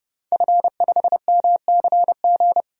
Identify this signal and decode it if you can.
Morse code, heard as a keyed tone, about 30 wpm, 700 Hz. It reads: F5MCG
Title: DXCC-100 Call Signs - 5 Characters - 30wpm